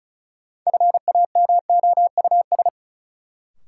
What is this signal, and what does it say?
Morse code, heard as a keyed tone, about 36 wpm, 700 Hz. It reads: FAMOUS
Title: Sentences from Top 1500 Words - New Word Drill - 35wpm